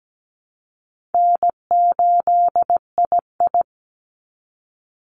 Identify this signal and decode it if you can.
Morse code, heard as a keyed tone, about 17 wpm, 700 Hz. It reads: N8II
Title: Easy Call Signs Repeated 2x - 17wpm